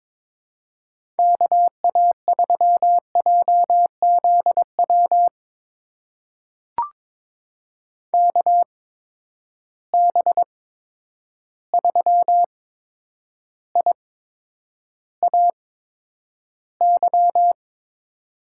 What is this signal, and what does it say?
Morse code, heard as a keyed tone, about 22 wpm, spaced as for 9 wpm, 700 Hz. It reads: KA3JZW KB3IAY
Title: USA Call Signs - 6 Characters - 8x Character Spacing - 22wpm